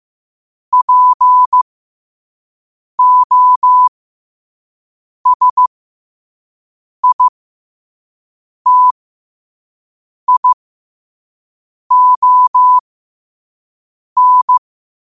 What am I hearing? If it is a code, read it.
Morse code, heard as a keyed tone, about 15 wpm, spaced as for 5 wpm, 1000 Hz. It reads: POSITION
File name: qotdfiles / 5wpm